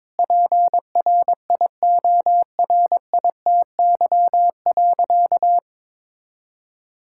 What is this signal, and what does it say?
Morse code, heard as a keyed tone, about 22 wpm, 700 Hz. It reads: PRIORITY.
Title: Sentences from Top 1600 Words 22wpm